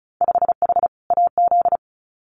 Morse code, heard as a keyed tone, about 36 wpm, 700 Hz. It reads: 5H AZ
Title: Winter Field Day Exchanges Encore 35wpm